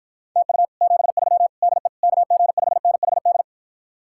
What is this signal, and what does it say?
Morse code, heard as a keyed tone, about 53 wpm, 700 Hz. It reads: TU 73 DE KG5NHD